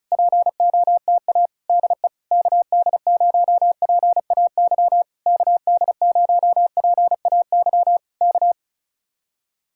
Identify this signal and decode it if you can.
Morse code, heard as a keyed tone, about 32 wpm, 700 Hz. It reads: POTA DE KD0PAY KD0PAY K
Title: Speed-Racing Common QSO Phrases Encore - 35wpm